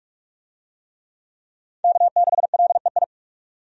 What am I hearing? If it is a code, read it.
Morse code, heard as a keyed tone, about 45 wpm, 700 Hz. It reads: K6LEI
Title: USA Call Signs - 5 Characters - 45wpm